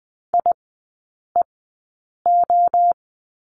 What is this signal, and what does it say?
Morse code, heard as a keyed tone, about 20 wpm, spaced as for 8 wpm, 700 Hz. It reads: IEO
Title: 035 Lesson 7e - Letter s - Three Characters 20wpm